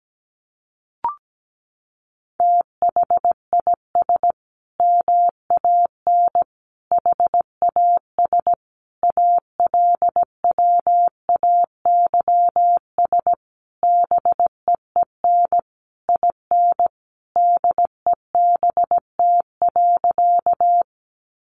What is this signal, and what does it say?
Morse code, heard as a keyed tone, about 17 wpm, 700 Hz. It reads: THIS MAN HAS ALWAYS BEEN IN DEBT.
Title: Sentences from Top 1300 Words 17wpm